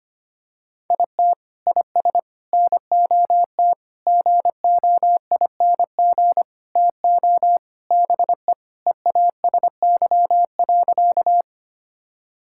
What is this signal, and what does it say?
Morse code, heard as a keyed tone, about 25 wpm, 700 Hz. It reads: IT IS NOT GOING TO BE EASY.
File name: practice / 25wpm / Sentences with 4-Word Phrases - 25wpm